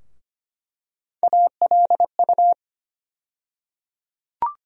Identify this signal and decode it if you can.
Morse code, heard as a keyed tone, about 25 wpm, 700 Hz. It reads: ALU
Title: USA Call Sign Postfixes - 3 Letters 25wpm